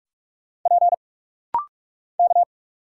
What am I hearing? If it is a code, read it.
Morse code, heard as a keyed tone, about 46 wpm, 700 Hz. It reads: P K
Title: Single Letter-Number Alt - Rapid-Fire 45wpm